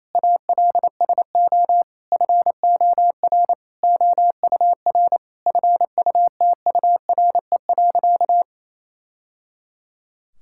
Morse code, heard as a keyed tone, about 28 wpm, 700 Hz. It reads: ALSO FOR OUR FUTURE.
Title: Sentences from Top 1000 Words - Review Part 6 - 28wpm